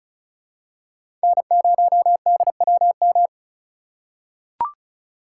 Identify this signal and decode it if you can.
Morse code, heard as a keyed tone, about 36 wpm, 700 Hz. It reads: N0DWM
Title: USA Call Signs - 5 Characters - 6x Character Spacing - 35wpm